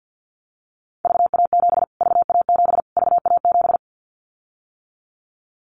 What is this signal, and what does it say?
Morse code, heard as a keyed tone, about 52 wpm, 700 Hz. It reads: VA7 VA7 VA7
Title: Callsign Prefixes for Top 25 DXCC - Repeated 3x - 50wpm